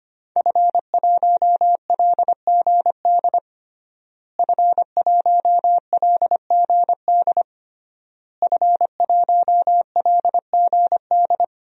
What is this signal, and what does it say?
Morse code, heard as a keyed tone, about 25 wpm, 700 Hz. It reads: F1LGD F1LGD F1LGD
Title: DXCC-100 Call Signs - 5 Characters - Repeated 3x - 25wpm